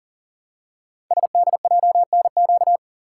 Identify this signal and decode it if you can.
Morse code, heard as a keyed tone, about 39 wpm, 700 Hz. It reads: SDJNQ
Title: Code Groups - 5 Letter:Numbers - 40wpm